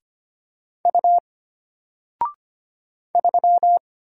Morse code, heard as a keyed tone, about 25 wpm, 700 Hz. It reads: U 3
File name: Single-Letter-Number-25wpm